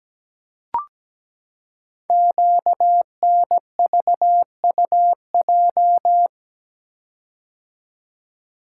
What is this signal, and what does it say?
Morse code, heard as a keyed tone, about 17 wpm, 700 Hz. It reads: QNVUJ
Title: Code Groups - 5 Letters - 17wpm